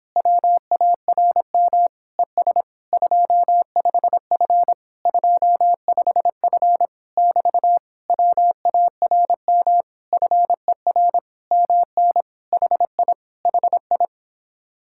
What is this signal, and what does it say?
Morse code, heard as a keyed tone, about 26 wpm, 700 Hz. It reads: WARM ES 25F 25F <BT> WARM FER MN HI HI